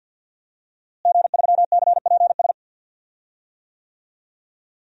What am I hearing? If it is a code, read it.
Morse code, heard as a keyed tone, about 52 wpm, 700 Hz. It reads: G3CPS
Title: DXCC-100 Call Signs - 5 Characters - Repeated 2x - 50wpm